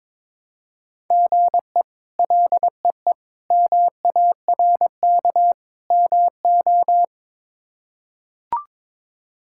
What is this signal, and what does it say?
Morse code, heard as a keyed tone, about 21 wpm, 700 Hz. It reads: GE LEE MARK MO
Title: K1USN SST QSO Practice - 22wpm